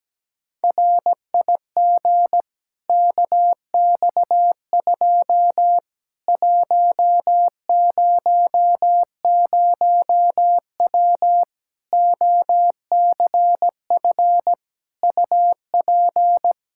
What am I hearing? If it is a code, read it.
Morse code, heard as a keyed tone, about 17 wpm, 700 Hz. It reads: RIG KX2 100W OCF UP
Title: Practice QSO - Exchange 2 - Info - 17wpm